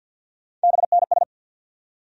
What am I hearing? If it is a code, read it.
Morse code, heard as a keyed tone, about 49 wpm, 700 Hz. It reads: BNS
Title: USA Call Sign Postfixes - 3 Letters 50wpm